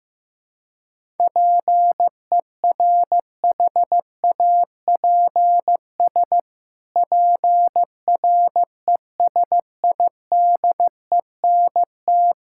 Morse code, heard as a keyed tone, about 16 wpm, 700 Hz. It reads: PERHAPS PRESIDENT
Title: Pangram Challenge v2 - 15wpm